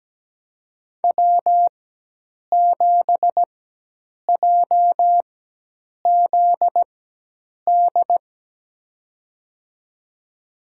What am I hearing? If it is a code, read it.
Morse code, heard as a keyed tone, about 17 wpm, spaced as for 8 wpm, 700 Hz. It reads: W7JZD
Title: USA Call Signs - 5 Characters - 4x Character Spacing - 17wpm